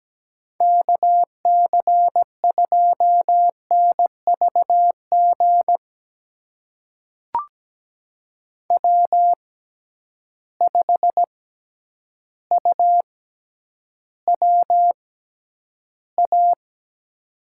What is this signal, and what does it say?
Morse code, heard as a keyed tone, about 17 wpm, spaced as for 9 wpm, 700 Hz. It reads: KC2NVG W5UWA
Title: USA Call Signs - 6x Character Spacing - 17wpm